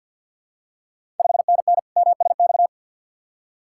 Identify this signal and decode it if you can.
Morse code, heard as a keyed tone, about 52 wpm, 690 Hz. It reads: 5NN MSX
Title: Ontario QSO Party Exchange 50wpm